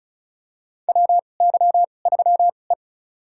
Morse code, heard as a keyed tone, about 36 wpm, spaced as for 25 wpm, 700 Hz. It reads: WY3E
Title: USA Call Signs - 2x Character Spacing - 35wpm